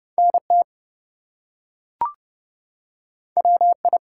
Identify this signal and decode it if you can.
Morse code, heard as a keyed tone, about 30 wpm, 700 Hz. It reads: NT WI
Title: US States and Canadian Provinces 30wpm